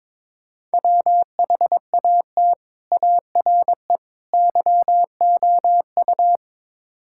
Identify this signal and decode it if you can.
Morse code, heard as a keyed tone, about 22 wpm, 700 Hz. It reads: WHAT ARE YOU
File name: practice / 22wpm / Build A Phrase - 5-Word Phrases - 22wpm